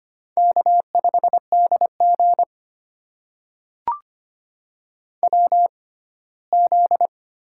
Morse code, heard as a keyed tone, about 25 wpm, spaced as for 17 wpm, 700 Hz. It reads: K5DG WZ
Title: USA Call Signs - 4 Characters - 6x Character Spacing - 25wpm